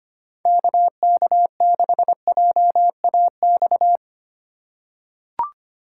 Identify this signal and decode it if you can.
Morse code, heard as a keyed tone, about 25 wpm, 700 Hz. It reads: KK6JAX
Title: USA Call Signs - 6 Characters - 4x Character Spacing - 25wpm